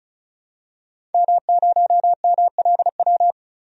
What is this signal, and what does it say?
Morse code, heard as a keyed tone, about 34 wpm, 700 Hz. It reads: M0MLW
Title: DXCC-100 Call Signs - 5 Characters - 35wpm